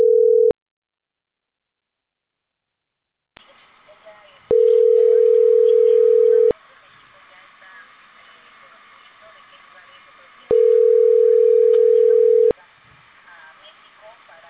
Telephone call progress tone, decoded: ringback tone

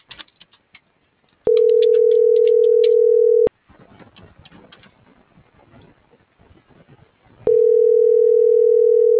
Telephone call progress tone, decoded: ringback tone

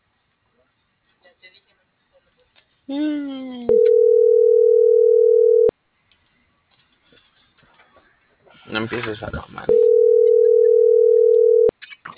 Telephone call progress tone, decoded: ringback tone

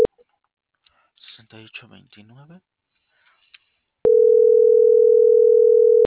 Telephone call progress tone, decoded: ringback tone

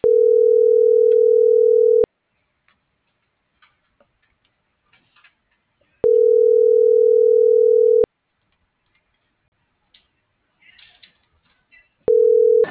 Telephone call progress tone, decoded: ringback tone

